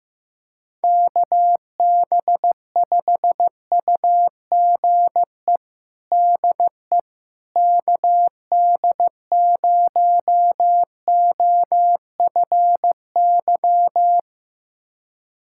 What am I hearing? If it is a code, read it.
Morse code, heard as a keyed tone, about 16 wpm, 700 Hz. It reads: KB5UGE DE KD0OFY
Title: USA Callsign DE Callsign 15wpm